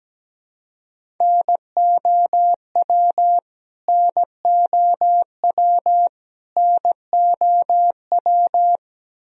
Morse code, heard as a keyed tone, about 17 wpm, 700 Hz. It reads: NOW NOW NOW
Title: Top 100 Words Repeated 17wpm